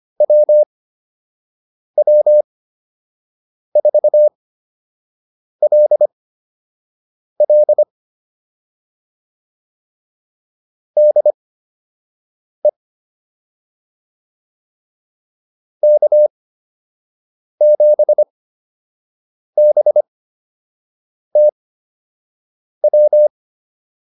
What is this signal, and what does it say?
Morse code, heard as a keyed tone, about 25 wpm, spaced as for 6 wpm, 600 Hz. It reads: WW4LL DE K7BTW